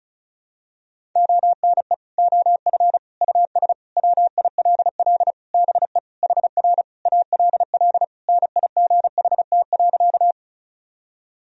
Morse code, heard as a keyed tone, about 36 wpm, 700 Hz. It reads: ONE OF US WILL BE HR ALL NIGHT.
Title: Sentences from Top 300 Words and Abbreviations 35wpm